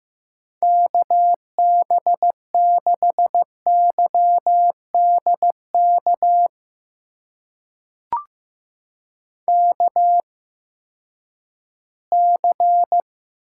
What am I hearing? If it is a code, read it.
Morse code, heard as a keyed tone, about 15 wpm, spaced as for 10 wpm, 700 Hz. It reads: KB6YDK KC
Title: USA Call Signs - 6 Characters - 8x Character Spacing - 15wpm